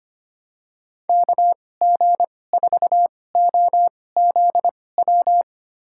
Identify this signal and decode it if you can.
Morse code, heard as a keyed tone, about 25 wpm, spaced as for 18 wpm, 700 Hz. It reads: KG4OZW
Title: USA Call Signs - 6 Characters - 2x Character Spacing - 25wpm